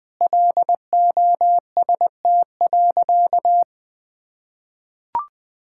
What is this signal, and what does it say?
Morse code, heard as a keyed tone, about 20 wpm, 700 Hz. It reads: LOST.
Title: Sentences with 5-Word Phrases - 20wpm